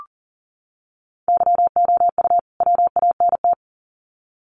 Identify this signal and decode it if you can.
Morse code, heard as a keyed tone, about 40 wpm, 700 Hz. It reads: YOU WANT
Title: Sets of 2 Words 40wpm